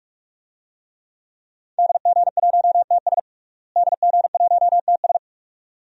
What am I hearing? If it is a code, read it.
Morse code, heard as a keyed tone, about 42 wpm, 700 Hz. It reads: DG1TS DG1TS
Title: DXCC-100 Call Signs - 5 Characters - Repeated 2x - 45wpm